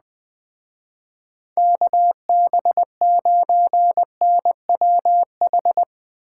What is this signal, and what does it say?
Morse code, heard as a keyed tone, about 20 wpm, 700 Hz. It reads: KB9NWH
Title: Speed Racing - USA 2x3 Call Signs 17wpm